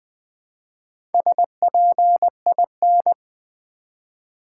Morse code, heard as a keyed tone, about 20 wpm, 700 Hz. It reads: SPIN
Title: Build A Word - 2 Letters At A Time - 4-Letter Words - 20wpm